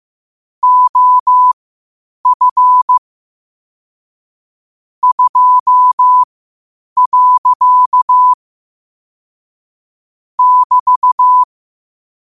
Morse code, heard as a keyed tone, about 15 wpm, spaced as for 8 wpm, 1000 Hz. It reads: OF 2. <BT>
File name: qotdfiles / 8wpm